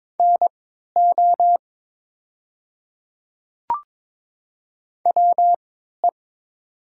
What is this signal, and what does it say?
Morse code, heard as a keyed tone, about 22 wpm, spaced as for 12 wpm, 700 Hz. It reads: NO WE
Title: 3x Character Spacing - Top 100 Words - 22wpm